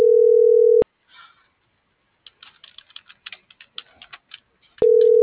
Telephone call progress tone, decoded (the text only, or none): ringback tone